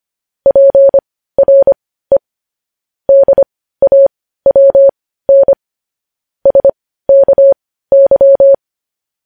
Morse code, heard as a keyed tone, about 25 wpm, spaced as for 15 wpm, 550 Hz. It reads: PRE DAWN SKY
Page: news-headlines-in-morse-code-at-15-wpm